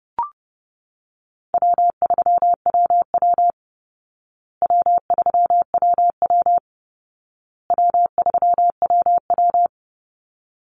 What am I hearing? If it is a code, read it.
Morse code, heard as a keyed tone, about 30 wpm, 700 Hz. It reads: W3WW W3WW W3WW
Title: Call Signs Repeated 2x - 30wpm